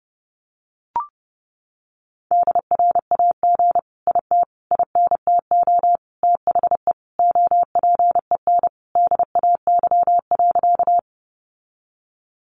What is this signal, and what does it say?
Morse code, heard as a keyed tone, about 28 wpm, 700 Hz. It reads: DRAG IT INTO THE OPEN DAY.
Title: Sentences from Top 1500 Words 30wpm